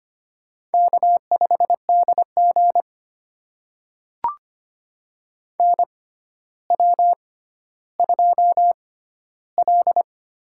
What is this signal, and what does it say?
Morse code, heard as a keyed tone, about 25 wpm, spaced as for 13 wpm, 700 Hz. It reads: K5DG NW2L
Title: USA Call Signs - 6x Character Spacing - 25wpm